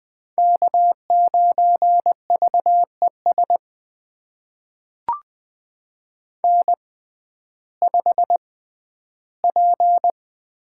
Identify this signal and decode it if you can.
Morse code, heard as a keyed tone, about 20 wpm, spaced as for 12 wpm, 700 Hz. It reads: K9VES N5P
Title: USA Call Signs - 5 Characters - 6x Character Spacing - 20wpm